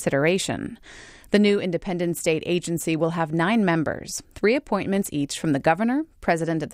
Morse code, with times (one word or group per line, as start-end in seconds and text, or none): none